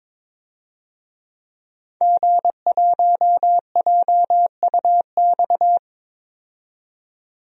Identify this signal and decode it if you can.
Morse code, heard as a keyed tone, about 22 wpm, 700 Hz. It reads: G1JUX
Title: DXCC-100 Call Signs - 5 Characters - 22wpm